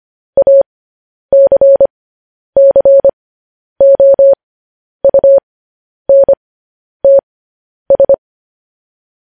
Morse code, heard as a keyed tone, about 25 wpm, spaced as for 10 wpm, 550 Hz. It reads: ACCOUNTS
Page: news headlines in morse code at 10 wpm